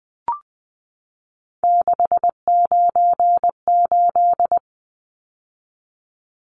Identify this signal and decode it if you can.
Morse code, heard as a keyed tone, about 20 wpm, 700 Hz. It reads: 698
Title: Numbers - 3-Digit 20wpm